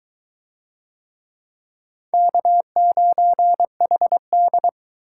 Morse code, heard as a keyed tone, about 22 wpm, 700 Hz. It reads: K9HD